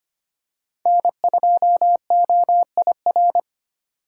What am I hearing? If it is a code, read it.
Morse code, heard as a keyed tone, about 25 wpm, 700 Hz. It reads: N2OIR
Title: USA Call Signs - 5 Characters - 2x Character Spacing - 25wpm